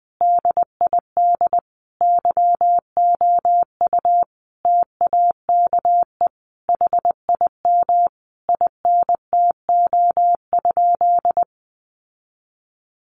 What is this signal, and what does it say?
Morse code, heard as a keyed tone, about 20 wpm, 700 Hz. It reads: DID YOU TAKE HIM INTO?
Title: Sentences from Top 600 Words 20wpm